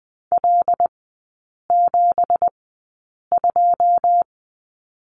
Morse code, heard as a keyed tone, about 20 wpm, spaced as for 8 wpm, 700 Hz. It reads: L72
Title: 184 Lesson 30e - Letter b - Three Characters 20wpm